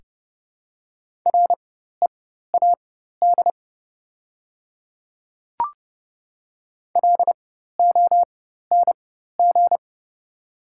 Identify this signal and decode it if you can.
Morse code, heard as a keyed tone, about 29 wpm, spaced as for 14 wpm, 690 Hz. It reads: READ LONG
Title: 4x Character Spacing - Top 200 Words - 30wpm